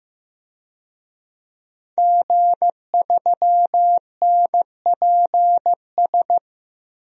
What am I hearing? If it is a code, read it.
Morse code, heard as a keyed tone, about 15 wpm, 700 Hz. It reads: G3NPS
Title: DXCC-100 Call Signs - 5 Characters - 15wpm